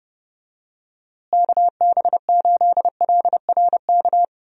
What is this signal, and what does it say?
Morse code, heard as a keyed tone, about 32 wpm, 700 Hz. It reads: KB8LRK
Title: USA Call Signs - 6 Characters - Repeated 3x - 30wpm